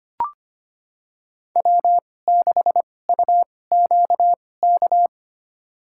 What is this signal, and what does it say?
Morse code, heard as a keyed tone, about 25 wpm, spaced as for 18 wpm, 700 Hz. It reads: W6UQK
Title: USA Call Signs - 5 Characters - 2x Character Spacing - 25wpm